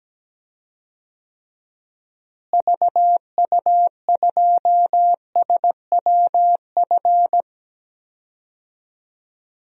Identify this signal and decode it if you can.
Morse code, heard as a keyed tone, about 17 wpm, 700 Hz. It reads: VU2SWF
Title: DXCC-100 Call Signs - 6 Characters - Repeated 2x - 17wpm